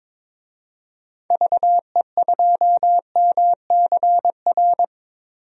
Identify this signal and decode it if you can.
Morse code, heard as a keyed tone, about 22 wpm, 700 Hz. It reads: VE2MCR
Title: DXCC-100 Call Signs - 6 Characters - 22wpm